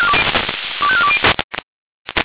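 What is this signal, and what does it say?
Five-tone selective call (ZVEI), heard as quick runs of five tones, tones of about 80 ms. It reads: 25420
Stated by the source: ZVEI five-tone selective call 25431, received by SDR.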